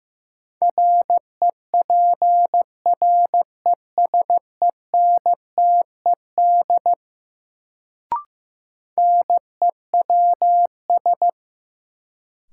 Morse code, heard as a keyed tone, about 16 wpm, 700 Hz. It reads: REPRESENTED NEWS
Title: Sets of 5 Words - New Word Drill - 15wpm